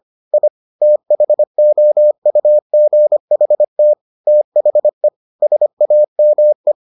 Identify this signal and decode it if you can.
Morse code, heard as a keyed tone, about 25 wpm, 600 Hz. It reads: I THOUGHT THE SAME